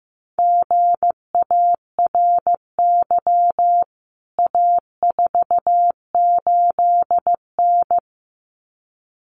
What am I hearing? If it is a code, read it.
Morse code, heard as a keyed tone, about 15 wpm, 700 Hz. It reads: GARY A48N
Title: CWT Exchange 15wpm